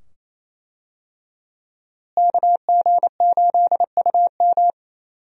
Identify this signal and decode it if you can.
Morse code, heard as a keyed tone, about 27 wpm, 700 Hz. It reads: KG8UM
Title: USA Call Signs - 5 Characters - Repeated 2x - 28wpm